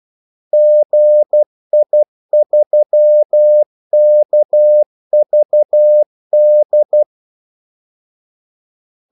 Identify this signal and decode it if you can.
Morse code, heard as a keyed tone, about 12 wpm, 600 Hz. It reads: GI3KVD
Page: 2 ekstra ordmellemrum mellem kaldesignalerne